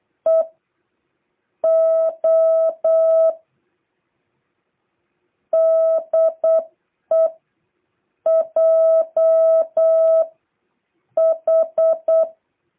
Morse code, heard as a keyed tone, about 8 wpm, spaced as for 6 wpm, 660 Hz. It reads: EO DE JH